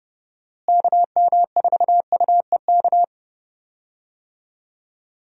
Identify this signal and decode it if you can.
Morse code, heard as a keyed tone, about 31 wpm, 700 Hz. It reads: KM4UEK
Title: USA Call Signs - 6 Characters - Repeated 3x - 30wpm